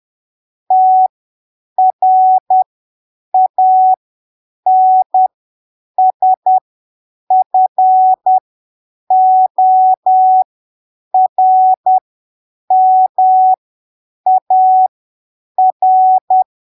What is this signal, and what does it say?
Morse code, heard as a keyed tone, about 10 wpm, spaced as for 7 wpm, 750 Hz. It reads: TRANSFORMAR
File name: arquivos / texto / texto 15 10ppm